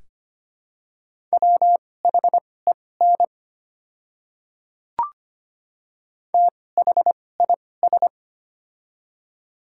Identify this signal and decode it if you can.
Morse code, heard as a keyed tone, about 25 wpm, spaced as for 18 wpm, 700 Hz. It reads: WHEN THIS
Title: 2x Character Spacing - Top 100 Words - 25wpm